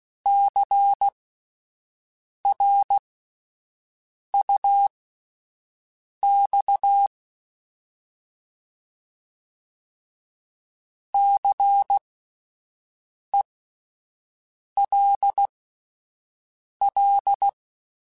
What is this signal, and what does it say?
Morse code, heard as a keyed tone, about 16 wpm, spaced as for 5 wpm, 800 Hz. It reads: CRUX CELL
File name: Alphabetical / Alphabetical mp3s / Cs @5